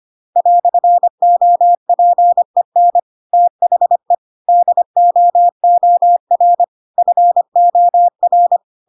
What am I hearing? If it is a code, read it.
Morse code, heard as a keyed tone, about 25 wpm, 700 Hz. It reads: "OPEN THE DOOR FOR
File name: Pinocchio / Pinocchio ch7 25 25 ik1hge